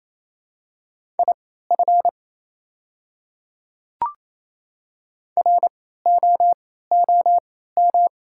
Morse code, heard as a keyed tone, about 28 wpm, spaced as for 16 wpm, 700 Hz. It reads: IF ROOM
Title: 3x Character Spacing - Top 300 Words - 28wpm